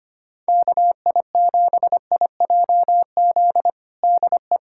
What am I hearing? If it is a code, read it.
Morse code, heard as a keyed tone, about 25 wpm, 700 Hz. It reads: KI7IJZ DE
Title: USA Callsign DE Callsign 25wpm